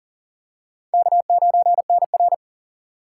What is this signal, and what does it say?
Morse code, heard as a keyed tone, about 40 wpm, 700 Hz. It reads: K9NR
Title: K1USN SST QSO Practice - 40wpm